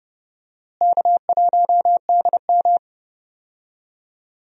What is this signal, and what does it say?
Morse code, heard as a keyed tone, about 32 wpm, 700 Hz. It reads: K1DM
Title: USA Call Signs - 4 Characters - Repeated 2x - 30wpm